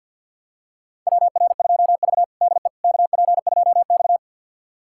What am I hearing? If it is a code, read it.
Morse code, heard as a keyed tone, about 52 wpm, 700 Hz. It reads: WR2V DE KP2X
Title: USA Callsign DE Callsign - 4 Characters - 50wpm